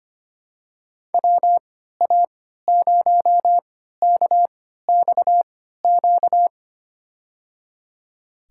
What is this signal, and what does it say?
Morse code, heard as a keyed tone, about 25 wpm, spaced as for 14 wpm, 700 Hz. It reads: WA0KXQ
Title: USA Call Signs - 3x Character Spacing - 25wpm